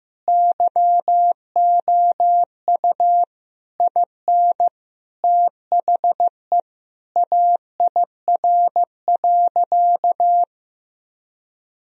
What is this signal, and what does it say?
Morse code, heard as a keyed tone, about 15 wpm, 700 Hz. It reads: YOU IN THE AIR.
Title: Sentences from Top 1000 Words - Review Part 1 - 15wpm